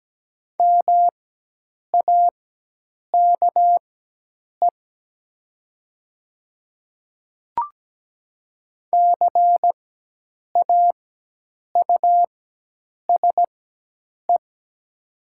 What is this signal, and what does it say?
Morse code, heard as a keyed tone, about 17 wpm, spaced as for 8 wpm, 700 Hz. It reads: MAKE CAUSE
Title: 4x Character Spacing - Top 200 Words - 17wpm